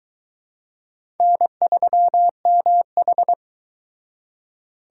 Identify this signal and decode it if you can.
Morse code, heard as a keyed tone, about 22 wpm, 700 Hz. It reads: N3MH